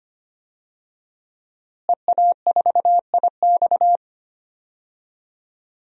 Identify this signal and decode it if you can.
Morse code, heard as a keyed tone, about 25 wpm, 700 Hz. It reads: EA4IX
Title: DXCC-100 Call Signs - 5 Characters - Repeated 3x - 25wpm